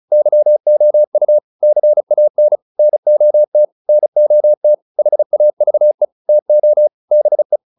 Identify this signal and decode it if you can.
Morse code, heard as a keyed tone, about 34 wpm, 600 Hz. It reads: YOU CAN NOT NOT HAVE TO BE